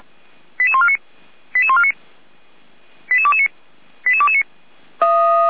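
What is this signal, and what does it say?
Five-tone selective call (ZVEI), heard as quick runs of five tones, tones of about 70 ms. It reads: 88149 88149 88229 88229
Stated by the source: ZVEI five-tone selective call received on SDR, Feuerwehr Orlamünde fire brigade.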